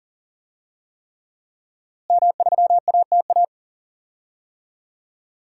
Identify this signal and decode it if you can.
Morse code, heard as a keyed tone, about 40 wpm, 700 Hz. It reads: M3ATA
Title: DXCC-100 Call Signs - 5 Characters - Repeated 3x - 40wpm